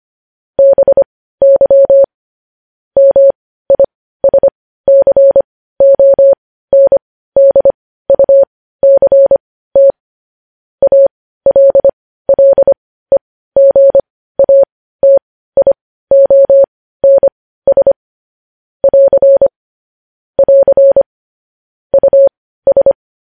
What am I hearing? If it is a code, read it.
Morse code, heard as a keyed tone, about 25 wpm, spaced as for 15 wpm, 550 Hz. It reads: BY MISCONDUCT ALLEGATIONS <AR> <AR> US
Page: news headlines in morse code at 15 wpm